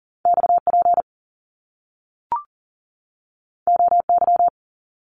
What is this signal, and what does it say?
Morse code, heard as a keyed tone, about 38 wpm, 700 Hz. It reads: XP OY
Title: Top Two Letter Combinations in Words 40wpm